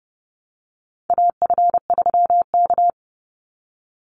Morse code, heard as a keyed tone, about 30 wpm, 700 Hz. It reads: AF3K
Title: Call Signs Repeated 2x - 30wpm